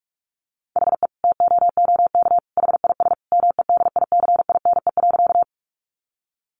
Morse code, heard as a keyed tone, about 44 wpm, 700 Hz. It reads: HE TOOK HIS MEDICINE.
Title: Sentences from Top 900 Words 45wpm